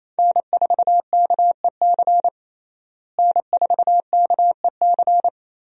Morse code, heard as a keyed tone, about 28 wpm, 700 Hz. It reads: N4KEC N4KEC
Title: USA Call Signs - 5 Characters - Repeated 3x - 28wpm